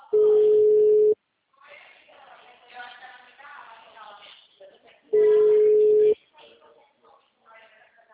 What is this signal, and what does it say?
Ringback tone (Europe, 425 Hz), with speech.